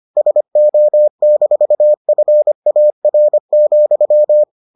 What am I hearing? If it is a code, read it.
Morse code, heard as a keyed tone, about 25 wpm, 600 Hz. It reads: SO-FAR,